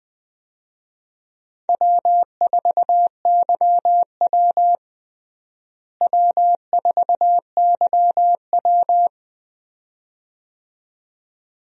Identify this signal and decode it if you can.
Morse code, heard as a keyed tone, about 20 wpm, 700 Hz. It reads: W4YW W4YW
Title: USA Call Signs - 4 Characters - Repeated 2x - 20wpm